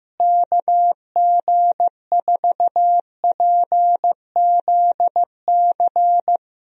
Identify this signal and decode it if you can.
Morse code, heard as a keyed tone, about 16 wpm, 700 Hz. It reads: KG4PZC
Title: USA Call Signs - 6 Characters - Repeated 3x - 15wpm